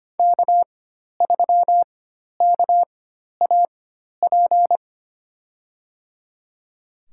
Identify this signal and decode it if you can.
Morse code, heard as a keyed tone, about 25 wpm, spaced as for 12 wpm, 700 Hz. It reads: K3KAP
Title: USA Call Signs - 5 Characters - 4x Character Spacing - 25wpm